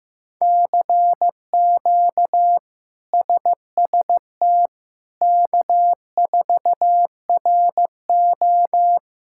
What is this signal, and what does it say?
Morse code, heard as a keyed tone, about 16 wpm, 700 Hz. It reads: CQ SST K4RO
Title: K1USN SST QSO Practice - 15wpm